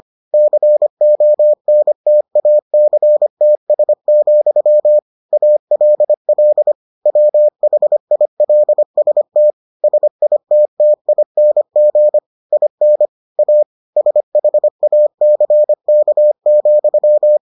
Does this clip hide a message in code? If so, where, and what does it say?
Yes: Morse code from 0.3 to 17.4 s, heard as a keyed tone, about 25 wpm, 600 Hz. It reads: CONTACTS, ALL WHILST SITTING IN A SHACK,